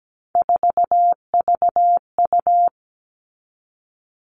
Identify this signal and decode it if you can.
Morse code, heard as a keyed tone, about 17 wpm, 700 Hz. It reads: 4VU
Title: All Cut Numbers - 3-Digit 17wpm